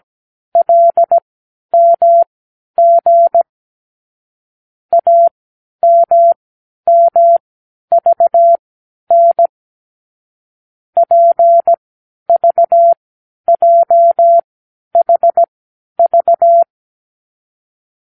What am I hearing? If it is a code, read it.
Morse code, heard as a keyed tone, about 17 wpm, spaced as for 10 wpm, 680 Hz. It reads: LMG AMMVN PVJHV
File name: k7qo / Lesson 36 (A-X)